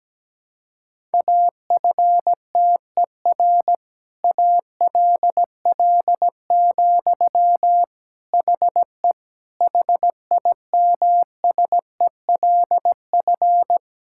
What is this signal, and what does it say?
Morse code, heard as a keyed tone, about 17 wpm, 700 Hz. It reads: AFTER ALL, HE HIMSELF
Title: Sentences from Top 1600 Words 17wpm